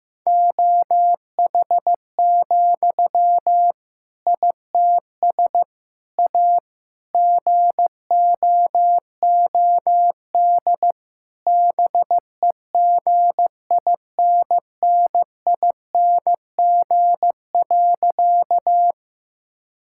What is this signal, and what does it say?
Morse code, heard as a keyed tone, about 16 wpm, 700 Hz. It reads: OH, ITS A GOOD BEGINNING.
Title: Sentences from Top 1000 Words - Review Part 5 - 15wpm